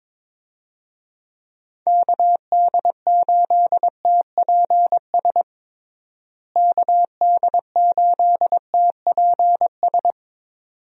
Morse code, heard as a keyed tone, about 22 wpm, 700 Hz. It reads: KD8TPS KD8TPS
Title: USA Call Signs - 6 Characters - Repeated 2x - 22wpm